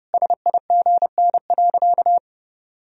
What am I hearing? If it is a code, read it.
Morse code, heard as a keyed tone, about 30 wpm, 700 Hz. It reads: SIGN.
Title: Sentences from Top 1000 Words - Review Part 4 - 30wpm